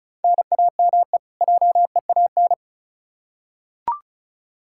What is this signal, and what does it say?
Morse code, heard as a keyed tone, about 34 wpm, 700 Hz. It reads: NAME JEAN
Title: Practice QSO - Exchange 1 - Beginning - 35wpm